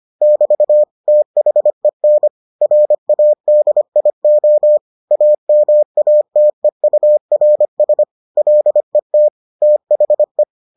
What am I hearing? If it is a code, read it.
Morse code, heard as a keyed tone, about 25 wpm, 600 Hz. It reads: <BT> THEN RADIO AMATEURS LET THE